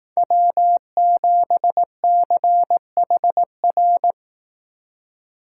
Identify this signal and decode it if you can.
Morse code, heard as a keyed tone, about 18 wpm, 700 Hz. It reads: W7CHR
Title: Speed Racing - USA 1x3 Call Signs 15wpm